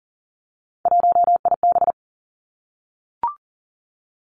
Morse code, heard as a keyed tone, about 38 wpm, 690 Hz. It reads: 1IB
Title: Three Challenging Letter-Number Combinations 40wpm